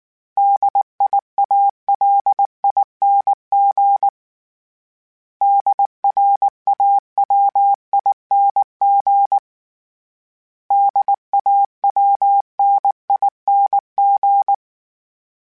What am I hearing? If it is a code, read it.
Morse code, heard as a keyed tone, about 19 wpm, 800 Hz. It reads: DIALING DRAWING DAWNING